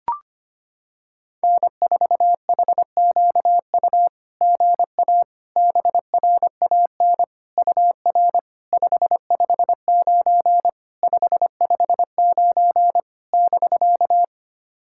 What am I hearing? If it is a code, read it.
Morse code, heard as a keyed tone, about 25 wpm, 700 Hz. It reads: N4HQU GA BRAN UR 559 559 <BK>
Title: SOTA QSO Practice 25wpm